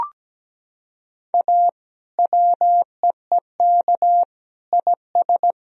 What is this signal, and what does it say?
Morse code, heard as a keyed tone, about 17 wpm, 700 Hz. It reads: A WEEK IS
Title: Sentences from Top 1000 Words - Review Part 2 - 17wpm